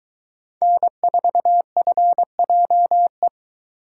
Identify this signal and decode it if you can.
Morse code, heard as a keyed tone, about 22 wpm, 700 Hz. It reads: N4FJE